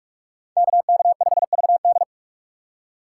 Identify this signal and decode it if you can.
Morse code, heard as a keyed tone, about 44 wpm, 700 Hz. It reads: KK5VD